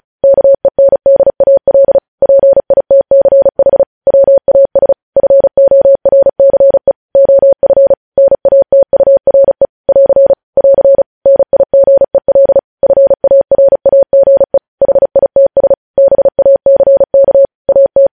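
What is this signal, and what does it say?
Morse code, heard as a keyed tone, about 36 wpm, 550 Hz. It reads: KENDAL PITCH WAS FORCE OF NATURE <AR> <AR> NIGEL FARAGE HITS BACK AT
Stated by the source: News headlines sent in Morse code at 35 WPM.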